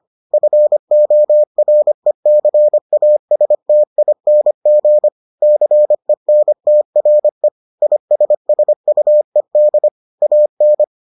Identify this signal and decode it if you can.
Morse code, heard as a keyed tone, about 25 wpm, 600 Hz. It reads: FORECASTING CENTRE ISSUED AN